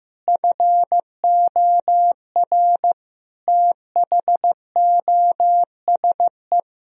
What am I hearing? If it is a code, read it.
Morse code, heard as a keyed tone, about 15 wpm, 700 Hz. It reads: FOR THOSE
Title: Sentences from Top 1000 Words - Review Part 1 - 15wpm